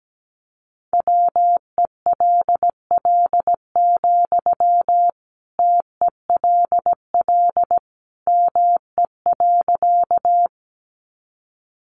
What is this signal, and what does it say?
Morse code, heard as a keyed tone, about 17 wpm, 700 Hz. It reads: WELL, TELL ME.
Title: Sentences from Top 200 Words 17wpm